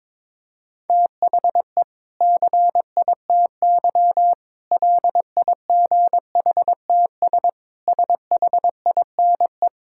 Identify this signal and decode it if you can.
Morse code, heard as a keyed tone, about 22 wpm, 700 Hz. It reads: THE CITY LIGHTS SHINE
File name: Difficult Sentences with Hints - 22wpm